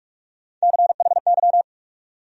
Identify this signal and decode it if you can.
Morse code, heard as a keyed tone, about 46 wpm, 700 Hz. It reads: CHY
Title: USA Call Sign Postfixes - 3 Letters 45wpm